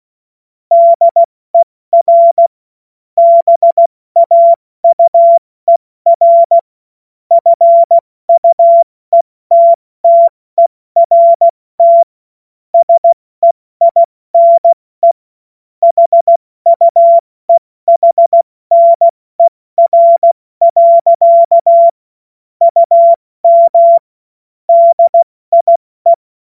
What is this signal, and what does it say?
Morse code, heard as a keyed tone, about 16 wpm, 680 Hz. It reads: DER BAUER FUETTERT SEINE HUEHNER. UM DIE
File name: Witz-15-Juli-2025-70-F80